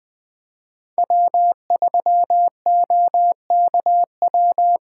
Morse code, heard as a keyed tone, about 20 wpm, 700 Hz. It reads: W3OKW